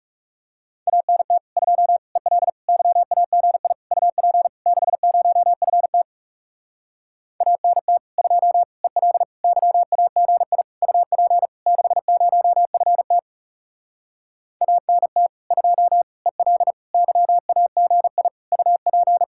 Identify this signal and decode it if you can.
Morse code, heard as a keyed tone, about 37 wpm, 700 Hz. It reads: ANT 2 EL YAGI UP 60FT ANT 2 EL YAGI UP 60FT ANT 2 EL YAGI UP
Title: Speed-Racing Common QSO Phrases - 30wpm